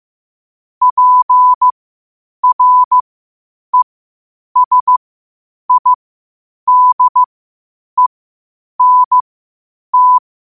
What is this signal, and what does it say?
Morse code, heard as a keyed tone, about 15 wpm, spaced as for 8 wpm, 1000 Hz. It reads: PRESIDENT
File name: qotdfiles / 8wpm